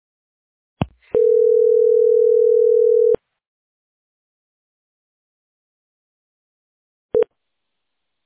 Ringback tone (North America), 440 + 480 Hz.